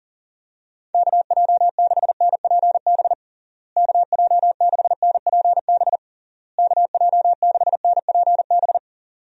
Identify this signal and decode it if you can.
Morse code, heard as a keyed tone, about 37 wpm, 700 Hz. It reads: KJ6NPB KJ6NPB KJ6NPB
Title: USA Call Signs - 6 Characters - Repeated 3x - 40wpm